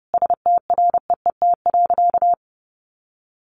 Morse code, heard as a keyed tone, about 30 wpm, 700 Hz. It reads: STREET.
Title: Sentences from Top 600 Words 30wpm